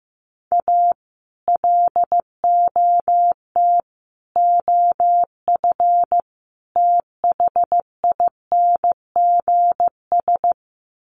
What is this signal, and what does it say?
Morse code, heard as a keyed tone, about 15 wpm, 700 Hz. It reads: A LOT OF THINGS
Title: Sets of 4 Words 15wpm